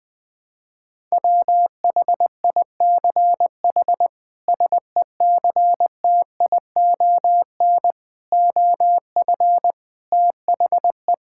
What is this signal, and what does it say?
Morse code, heard as a keyed tone, about 20 wpm, 700 Hz. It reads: WHICH SECTION OF THE
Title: Sentences from Top 1000 Words - Review Part 5 - 20wpm